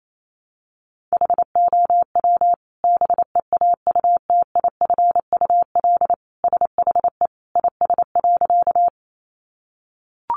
Morse code, heard as a keyed tone, about 28 wpm, 700 Hz. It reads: HOW BEAUTIFUL SHE IS.
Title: Sentences from Top 900 Words 28wpm